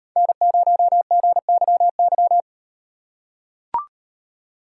Morse code, heard as a keyed tone, about 36 wpm, 700 Hz. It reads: N0GYY